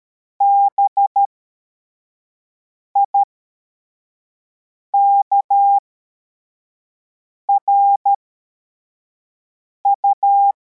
Morse code, heard as a keyed tone, about 13 wpm, spaced as for 4 wpm, 800 Hz. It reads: BIKRU
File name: formation / cw / Text43